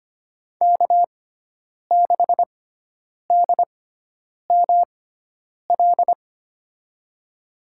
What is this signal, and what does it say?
Morse code, heard as a keyed tone, about 25 wpm, spaced as for 9 wpm, 700 Hz. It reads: K6DML
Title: USA Call Signs - 6x Character Spacing - 25wpm